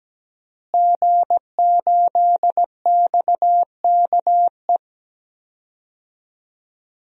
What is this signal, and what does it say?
Morse code, heard as a keyed tone, about 17 wpm, 700 Hz. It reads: G8XKE
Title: DXCC-100 Call Signs - 5 Characters - Repeated 2x - 17wpm